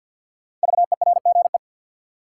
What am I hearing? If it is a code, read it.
Morse code, heard as a keyed tone, about 46 wpm, 700 Hz. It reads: VERGE
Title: Sets of 4 Words - New Word Drill - 50wpm